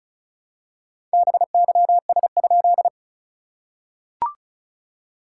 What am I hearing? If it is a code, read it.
Morse code, heard as a keyed tone, about 35 wpm, 700 Hz. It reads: BYS?
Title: SOTA QSO Practice 35wpm